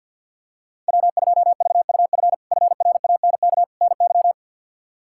Morse code, heard as a keyed tone, about 51 wpm, 700 Hz. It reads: W2VUF FRANK NY
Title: K1USN SST QSO Practice - 50wpm